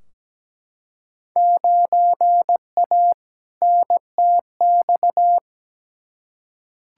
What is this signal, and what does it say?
Morse code, heard as a keyed tone, about 17 wpm, 700 Hz. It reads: 9A NTX
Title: Field Day Exchanges Encore - 17wpm